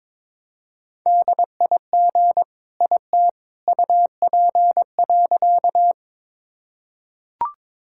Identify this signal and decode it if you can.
Morse code, heard as a keyed tone, about 22 wpm, 700 Hz. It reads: DIG IT UP.
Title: Sentences from Top 1600 Words 22wpm